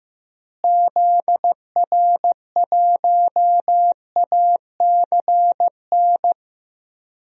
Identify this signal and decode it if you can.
Morse code, heard as a keyed tone, about 15 wpm, 700 Hz. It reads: ZR1ACN
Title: DXCC-100 Call Signs - 6 Characters - Repeated 3x - 15wpm